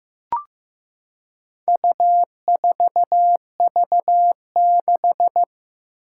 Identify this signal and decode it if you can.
Morse code, heard as a keyed tone, about 16 wpm, 700 Hz. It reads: U4V6
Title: Four Challenging Letter-Number Combinations 15wpm